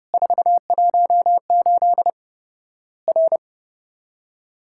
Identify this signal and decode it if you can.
Morse code, heard as a keyed tone, about 30 wpm, 700 Hz. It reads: 418 R
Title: Worked All Europe DX Contest - QTC Exchange - 30wpm